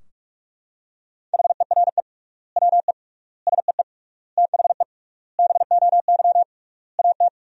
Morse code, heard as a keyed tone, about 46 wpm, 700 Hz. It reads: HERE WE SEE THE BOY AT
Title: 3x Word-Spacing - Sentences from Top 300 Words - 45wpm